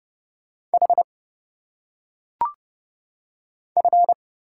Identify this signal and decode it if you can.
Morse code, heard as a keyed tone, about 30 wpm, 700 Hz. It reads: H F
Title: Single Letters - Phonetic Pronunciation 30wpm